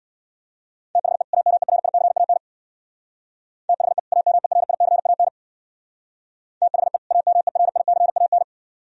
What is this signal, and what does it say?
Morse code, heard as a keyed tone, about 63 wpm, 700 Hz. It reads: THE AMERICAN THE AMERICAN THE AMERICAN